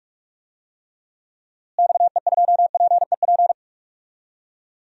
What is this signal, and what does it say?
Morse code, heard as a keyed tone, about 45 wpm, 700 Hz. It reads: XE2PEP
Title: DXCC-100 Call Signs - 6 Characters - Repeated 2x - 45wpm